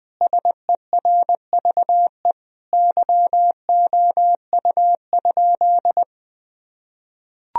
Morse code, heard as a keyed tone, about 20 wpm, 700 Hz. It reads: SERVE YOU?
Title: Sentences from Top 1000 Words - Review Part 2 - 20wpm